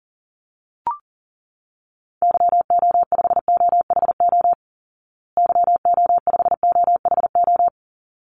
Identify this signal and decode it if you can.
Morse code, heard as a keyed tone, about 40 wpm, 700 Hz. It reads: YO5OHO YO5OHO
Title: Call Signs Repeated 3x - 40wpm